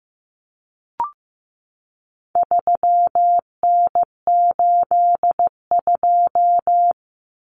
Morse code, heard as a keyed tone, about 15 wpm, 700 Hz. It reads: 3N82
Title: Common Cut Numbers - 4-Digit 15wpm